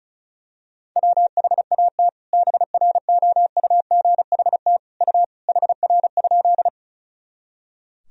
Morse code, heard as a keyed tone, about 36 wpm, 700 Hz. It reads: WHAT BROUGHT U HR?
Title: Sentences from Top 500 Words and Abbreviations 35wpm